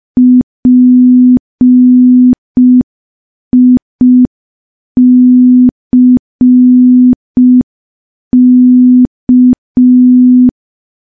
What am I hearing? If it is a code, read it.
Morse code, heard as a keyed tone, about 5 wpm, 250 Hz. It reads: PICK
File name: morse / morse-training-05WPM